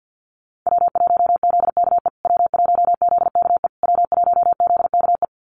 Morse code, heard as a keyed tone, about 50 wpm, 700 Hz. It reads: W1ZKE W1ZKE W1ZKE
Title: USA Call Signs Repeated 3x 50wpm